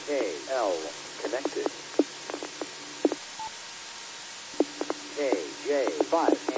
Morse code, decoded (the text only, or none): none